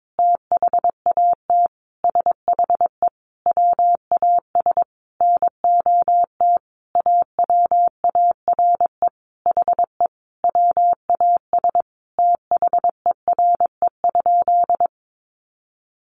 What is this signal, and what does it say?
Morse code, heard as a keyed tone, about 22 wpm, 700 Hz. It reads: THAT SHE WAS NOT AWARE HE WAS THERE?
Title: Sentences from Top 1200 Words 22wpm